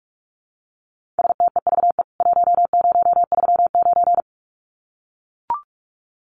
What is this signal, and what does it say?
Morse code, heard as a keyed tone, about 44 wpm, 700 Hz. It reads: STEVE 1039
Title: CWT Exchange 45wpm